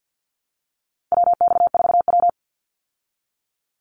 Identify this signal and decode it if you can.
Morse code, heard as a keyed tone, about 50 wpm, 700 Hz. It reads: WX4W
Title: Call Signs Repeated 3x - 50wpm